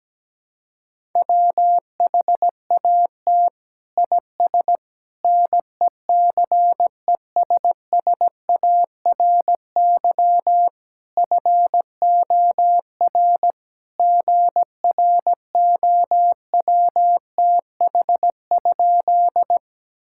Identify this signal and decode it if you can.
Morse code, heard as a keyed tone, about 17 wpm, 700 Hz. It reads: WHAT IS NECESSARY FOR GROWTH?
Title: Sentences from Top 1000 Words - Review Part 5 - 17wpm